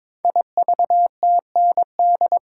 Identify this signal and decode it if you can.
Morse code, heard as a keyed tone, about 22 wpm, 700 Hz. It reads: IVTND
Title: Code Groups - 5 Letters - 22wpm